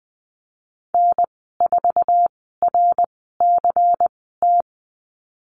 Morse code, heard as a keyed tone, about 20 wpm, spaced as for 15 wpm, 700 Hz. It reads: N4RCT
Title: 128 Lesson 21g - Number 6 - Call Signs 20wpm